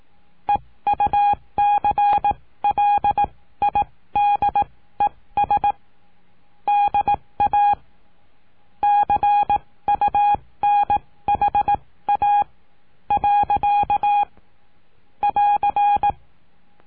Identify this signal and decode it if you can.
Morse code, heard as a keyed tone, about 18 wpm, spaced as for 14 wpm, 820 Hz. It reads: EUCLIDES DA CUNHA. <AR>
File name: cursos / CURSO DE TELEGRAFIA / Telegrafia - Trilha 19